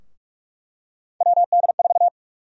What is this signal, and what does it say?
Morse code, heard as a keyed tone, about 42 wpm, 700 Hz. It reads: WD4
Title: USA Call Sign Prefixes - Repeated 2x - 45wpm